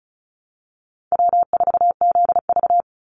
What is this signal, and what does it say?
Morse code, heard as a keyed tone, about 34 wpm, 700 Hz. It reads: W4ZV
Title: NA SOTA Chaser Callsigns 35wpm